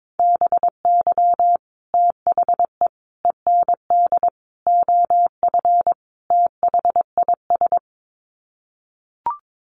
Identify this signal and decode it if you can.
Morse code, heard as a keyed tone, about 22 wpm, 700 Hz. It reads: BY THE END OF THIS
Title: Sets of 5 Words Encore 22wpm